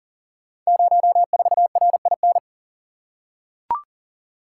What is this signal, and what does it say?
Morse code, heard as a keyed tone, about 40 wpm, 700 Hz. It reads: 04RIN
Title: Code Groups - 5 Letter:Numbers - 40wpm